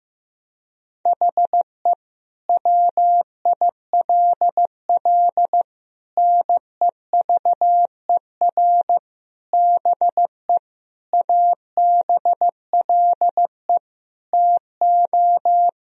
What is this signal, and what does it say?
Morse code, heard as a keyed tone, about 15 wpm, 700 Hz. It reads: HE WILL NEVER BE ABLE TO
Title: Sentences from Top 1000 Words - Review Part 2 - 15wpm